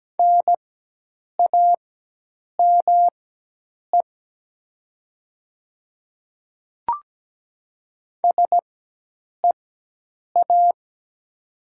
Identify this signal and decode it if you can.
Morse code, heard as a keyed tone, about 17 wpm, spaced as for 8 wpm, 700 Hz. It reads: NAME SEA
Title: 4x Character Spacing - Top 300 Words - 17wpm